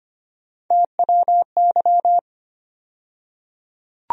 Morse code, heard as a keyed tone, about 25 wpm, 700 Hz. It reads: TWY
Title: USA Call Sign Postfixes - 3 Letters 25wpm